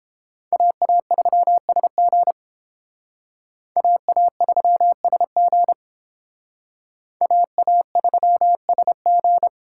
Morse code, heard as a keyed tone, about 31 wpm, 700 Hz. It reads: AA3SG AA3SG AA3SG